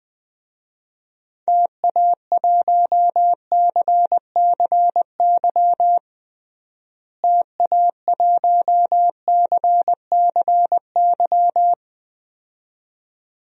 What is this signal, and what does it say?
Morse code, heard as a keyed tone, about 20 wpm, 700 Hz. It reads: TA1CCY TA1CCY
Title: DXCC-100 Call Signs - 6 Characters - Repeated 2x - 20wpm